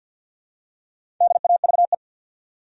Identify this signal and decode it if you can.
Morse code, heard as a keyed tone, about 51 wpm, 700 Hz. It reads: DAVE
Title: USA Common Names - 50wpm